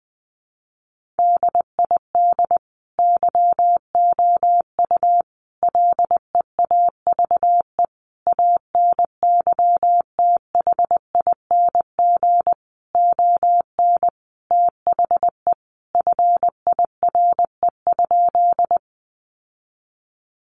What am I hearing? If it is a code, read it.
Morse code, heard as a keyed tone, about 20 wpm, 700 Hz. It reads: DID YOU LEAVE ANYTHING ON THE FIRE?
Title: Sentences from Top 1000 Words 20wpm